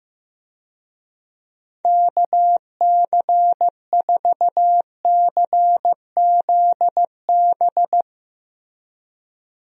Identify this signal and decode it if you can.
Morse code, heard as a keyed tone, about 15 wpm, 700 Hz. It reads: KC4CZB
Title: USA Call Signs - 6 Characters - Repeated 2x - 15wpm